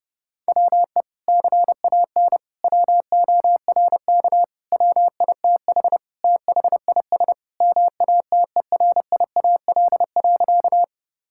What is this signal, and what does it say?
Morse code, heard as a keyed tone, about 30 wpm, 700 Hz. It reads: WE CAN WORK WITH THIS MATERIAL.
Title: Sentences from Top 1000 Words - Review Part 5 - 30wpm